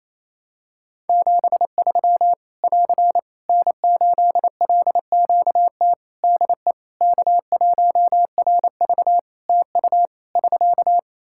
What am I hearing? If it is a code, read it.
Morse code, heard as a keyed tone, about 28 wpm, 700 Hz. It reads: 73 <AR> N8LQT DE K1RV TU <SK>
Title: K1USN SST QSO Practice 28wpm